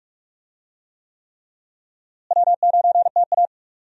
Morse code, heard as a keyed tone, about 46 wpm, 700 Hz. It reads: W9TA
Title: USA Call Signs - 4 Characters - Repeated 3x - 45wpm